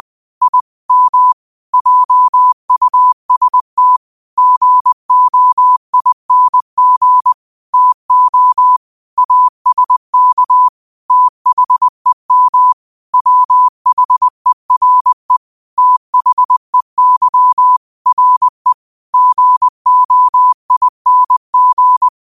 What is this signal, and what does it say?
Morse code, heard as a keyed tone, about 20 wpm, 1000 Hz. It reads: I M JUST GOING TO ASK THEM WHERE THEY RE GOING